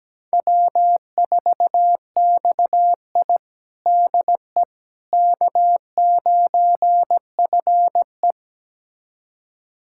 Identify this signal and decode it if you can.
Morse code, heard as a keyed tone, about 17 wpm, 700 Hz. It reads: W4XI DE K9FE
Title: USA Callsign DE Callsign - 4 Characters - 17wpm